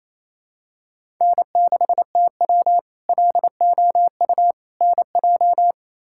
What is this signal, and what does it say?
Morse code, heard as a keyed tone, about 28 wpm, 700 Hz. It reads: N6TW LOU NJ
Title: K1USN SST QSO Practice - 28wpm